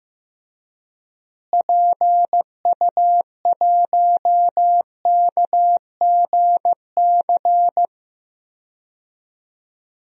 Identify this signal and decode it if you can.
Morse code, heard as a keyed tone, about 15 wpm, 700 Hz. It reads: PU1KGC
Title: DXCC-100 Call Signs - 6 Characters - Repeated 3x - 15wpm